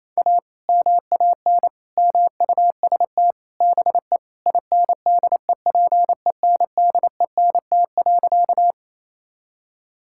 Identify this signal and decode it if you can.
Morse code, heard as a keyed tone, about 28 wpm, 700 Hz. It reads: A MAN MUST BE INDEPENDENT.
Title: Sentences from Top 1000 Words - Review Part 6 - 28wpm